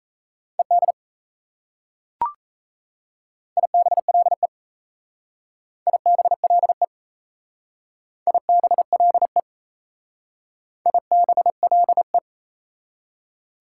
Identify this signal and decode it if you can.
Morse code, heard as a keyed tone, about 35 wpm, 700 Hz. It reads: ED IBLE IBLE IBLE IBLE
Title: Speed-Racing Prefixes and Suffixes 28wpm